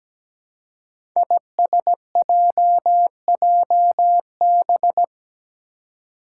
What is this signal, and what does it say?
Morse code, heard as a keyed tone, about 17 wpm, 700 Hz. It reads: ISJJB
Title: Five Challenging Letter-Number Combinations 17wpm